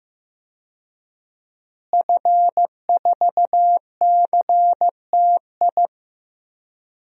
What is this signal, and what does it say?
Morse code, heard as a keyed tone, about 15 wpm, 700 Hz. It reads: F4CTI
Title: DXCC-100 Call Signs - 5 Characters - Repeated 3x - 15wpm